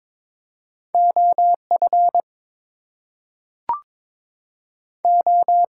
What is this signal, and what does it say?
Morse code, heard as a keyed tone, about 22 wpm, 700 Hz. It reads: OF O
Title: Two Letter Words - 22wpm